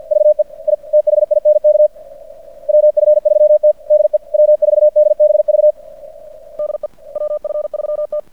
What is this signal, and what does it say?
Morse code, heard as a keyed tone, about 50 wpm, 600 Hz. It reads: VE I THINK WF3T DE W4DBV DE WF3T
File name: archived / CRSnet